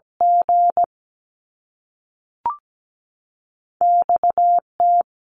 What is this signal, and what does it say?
Morse code, heard as a keyed tone, about 17 wpm, 700 Hz. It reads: G X T
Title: Top Two Letter Combinations in Words 17wpm